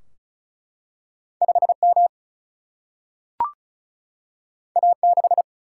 Morse code, heard as a keyed tone, about 36 wpm, 700 Hz. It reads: 5 M A 6
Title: Single-Letter Single-Number 35wpm